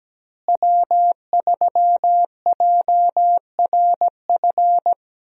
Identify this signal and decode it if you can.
Morse code, heard as a keyed tone, about 17 wpm, 700 Hz. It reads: W3JRF
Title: USA Call Signs - 5 Characters - Repeated 2x - 17wpm